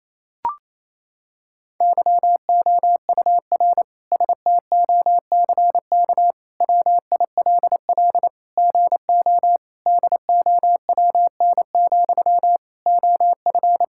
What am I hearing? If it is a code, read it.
Morse code, heard as a keyed tone, about 28 wpm, 700 Hz. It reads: YOUR STOCK WILL GO DOWN, OF
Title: Sentences from Top 1000 Words - Review Part 5 - 28wpm